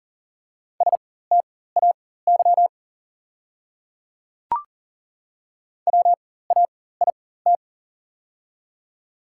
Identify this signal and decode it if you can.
Morse code, heard as a keyed tone, about 37 wpm, spaced as for 18 wpm, 690 Hz. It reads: STAY WAIT
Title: 4x Character Spacing - Top 500 Words - 40wpm